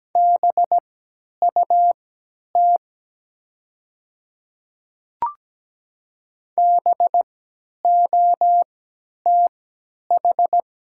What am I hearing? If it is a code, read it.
Morse code, heard as a keyed tone, about 17 wpm, spaced as for 10 wpm, 700 Hz. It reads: BUT BOTH
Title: 3x Character Spacing - Top 300 Words - 17wpm